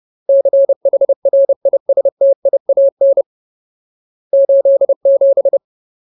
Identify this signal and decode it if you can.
Morse code, heard as a keyed tone, about 30 wpm, 550 Hz. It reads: CHRISTIAN 87
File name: Ultimate Word List / 30WPM Ultimate